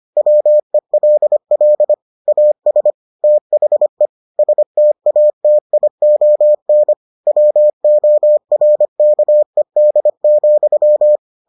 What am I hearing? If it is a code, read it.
Morse code, heard as a keyed tone, about 25 wpm, 600 Hz. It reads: WELL AS THE STATION WORKED,